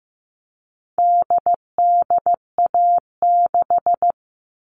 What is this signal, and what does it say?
Morse code, heard as a keyed tone, about 15 wpm, 700 Hz. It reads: DDA6
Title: All Cut Numbers - 4-Digit 15wpm